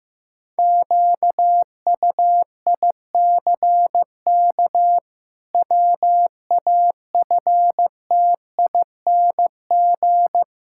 Morse code, heard as a keyed tone, about 16 wpm, 700 Hz. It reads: QUICK WAFTING
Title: Pangram Challenge v2 - 15wpm